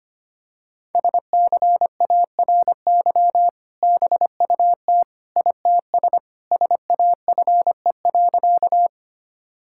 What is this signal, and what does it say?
Morse code, heard as a keyed tone, about 25 wpm, 700 Hz. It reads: SCARY BUT ITS SAFE.
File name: practice / 25wpm / Sentences with 2-Word Phrases - 25wpm